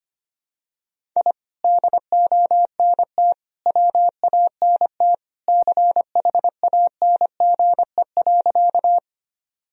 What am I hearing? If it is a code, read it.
Morse code, heard as a keyed tone, about 25 wpm, 700 Hz. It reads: I DONT WANT CHANGE.
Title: Sentences from Top 1000 Words - Review Part 1 - 25wpm